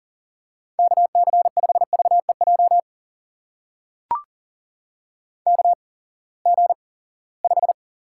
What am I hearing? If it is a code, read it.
Morse code, heard as a keyed tone, about 39 wpm, spaced as for 22 wpm, 700 Hz. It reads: KC5VEJ KG5
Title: USA Call Signs - 6 Characters - 8x Character Spacing - 40wpm